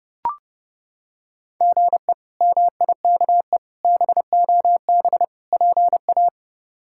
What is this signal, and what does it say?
Morse code, heard as a keyed tone, about 30 wpm, 700 Hz. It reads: GE MIKE BOB PA
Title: K1USN SST QSO Practice - 30wpm